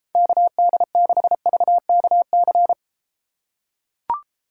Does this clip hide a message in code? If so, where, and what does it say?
Yes: Morse code from 0.2 to 2.7 s, heard as a keyed tone, about 33 wpm, 700 Hz. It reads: KD6VKC